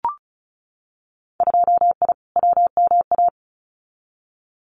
Morse code, heard as a keyed tone, about 34 wpm, 700 Hz. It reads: 2I WMA
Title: Winter Field Day Exchanges Encore 35wpm